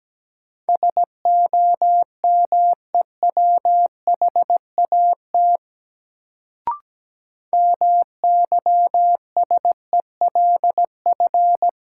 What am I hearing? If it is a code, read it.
Morse code, heard as a keyed tone, about 17 wpm, 700 Hz. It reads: SOMEWHAT MYSELF
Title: Sentences from Top 1300 Words - New Word Drill Alt - 17wpm